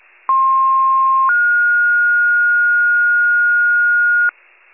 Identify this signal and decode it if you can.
Two-tone page, heard as a one-second tone, then a three-second tone, about 1060 then 1430 Hz, with noise. It is heard from 0.3 to 4.3 s.
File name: wcalert